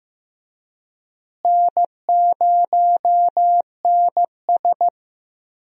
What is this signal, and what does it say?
Morse code, heard as a keyed tone, about 15 wpm, 700 Hz. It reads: N0NS
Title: USA Call Signs - 4 Characters - Repeated 2x - 15wpm